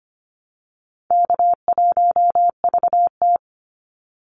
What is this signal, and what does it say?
Morse code, heard as a keyed tone, about 25 wpm, 700 Hz. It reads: K1VT
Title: Easy Call Signs Repeated 3x - 25wpm